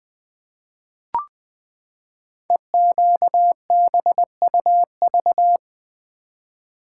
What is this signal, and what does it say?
Morse code, heard as a keyed tone, about 20 wpm, 700 Hz. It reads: EQBUV
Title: Code Groups - 5 Letters - 20wpm